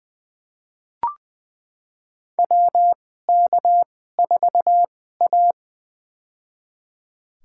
Morse code, heard as a keyed tone, about 20 wpm, spaced as for 14 wpm, 700 Hz. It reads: WK4A
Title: USA Call Signs - 4 Characters - 2x Character Spacing - 20wpm